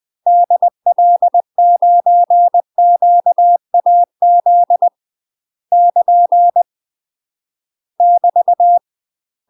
Morse code, heard as a keyed tone, about 20 wpm, 700 Hz. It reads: DL9QAZ <KN> <BT>
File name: MASTER QSOs TRAINING/20WPM